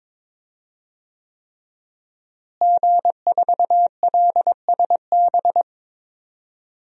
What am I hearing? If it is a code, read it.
Morse code, heard as a keyed tone, about 22 wpm, 700 Hz. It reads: G4LSB
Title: DXCC-100 Call Signs - 5 Characters - Repeated 3x - 22wpm